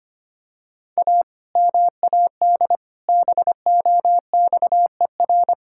Morse code, heard as a keyed tone, about 25 wpm, 700 Hz. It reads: A MAD BOXER
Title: Pangram Challenge v2 - 25wpm